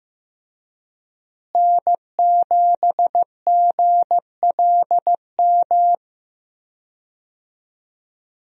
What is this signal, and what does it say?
Morse code, heard as a keyed tone, about 15 wpm, 700 Hz. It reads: N7GLM
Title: USA Call Signs - 5 Characters - Repeated 2x - 15wpm